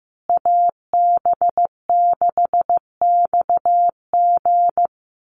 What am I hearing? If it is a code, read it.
Morse code, heard as a keyed tone, about 15 wpm, 700 Hz. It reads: AB6XG
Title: Call Signs Repeated 2x - 15wpm